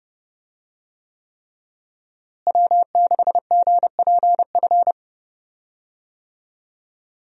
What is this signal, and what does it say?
Morse code, heard as a keyed tone, about 30 wpm, 700 Hz. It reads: W6GPF
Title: USA Call Signs - 5 Characters - Repeated 3x - 30wpm